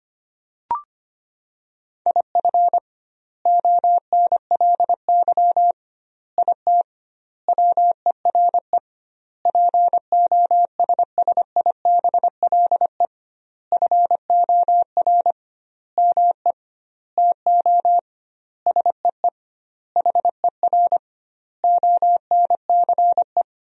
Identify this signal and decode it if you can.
Morse code, heard as a keyed tone, about 25 wpm, 700 Hz. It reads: IF ONLY IT WERE POSSIBLE FOR ME TO SEE HER ONCE
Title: 2x Word-Spacing - Sentences from Top 500 Words - 25wpm